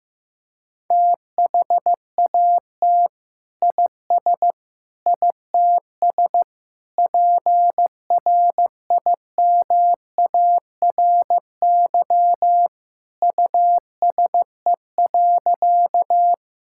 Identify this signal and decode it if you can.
Morse code, heard as a keyed tone, about 16 wpm, 700 Hz. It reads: THAT IS ITS PRIMARY USE.
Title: Sentences from Top 1000 Words - Review Part 5 - 15wpm